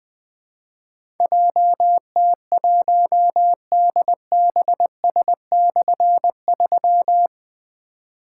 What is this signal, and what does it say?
Morse code, heard as a keyed tone, about 20 wpm, 700 Hz. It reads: JT1DBS/3
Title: Difficult Call Signs 20wpm